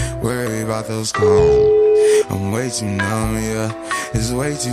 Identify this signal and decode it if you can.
Ringback tone (Europe, 425 Hz), with speech.